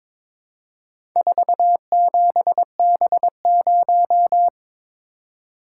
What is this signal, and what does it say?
Morse code, heard as a keyed tone, about 22 wpm, 700 Hz. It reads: 47B0
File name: practice / 22wpm / Four Letter-Number Combinations - 22wpm